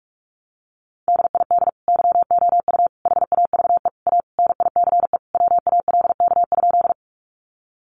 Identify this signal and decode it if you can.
Morse code, heard as a keyed tone, about 44 wpm, 700 Hz. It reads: DID YOU HAVE A NICE WALK?
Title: Sentences from Top 1100 Words 45wpm